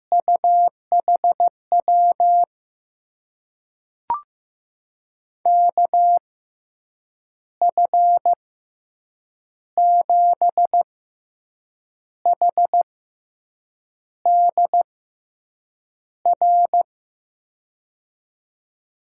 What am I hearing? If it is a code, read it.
Morse code, heard as a keyed tone, about 16 wpm, spaced as for 6 wpm, 700 Hz. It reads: UHW KF7HDR
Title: USA Call Signs - 6 Characters - 6x Character Spacing - 15wpm